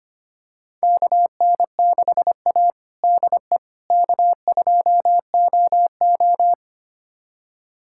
Morse code, heard as a keyed tone, about 25 wpm, 700 Hz. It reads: KN6A DE K2OO
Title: USA Callsign DE Callsign - 4 Characters - 25wpm